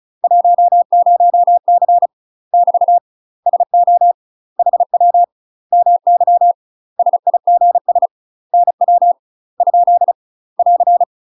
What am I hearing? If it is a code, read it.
Morse code, heard as a keyed tone, about 35 wpm, 700 Hz. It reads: 10C <BT> SO HW MY SIGS NW ? <AR>
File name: UK MASTER QSOs TRAINING / 35WPM / UK MASTER QSOs 0018